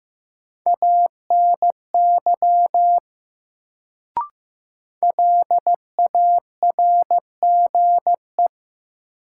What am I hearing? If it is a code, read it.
Morse code, heard as a keyed tone, about 15 wpm, 700 Hz. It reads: ANY LARGE
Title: Sentences from Top 200 Words - New Word Drill - 15wpm